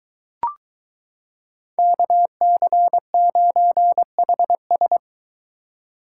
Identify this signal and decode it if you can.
Morse code, heard as a keyed tone, about 22 wpm, 700 Hz. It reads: KC9HS